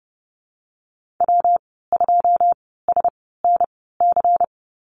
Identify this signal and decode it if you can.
Morse code, heard as a keyed tone, about 30 wpm, spaced as for 17 wpm, 700 Hz. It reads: W2SNC
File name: Lesson 19g - Letter w - Call Signs 30wpm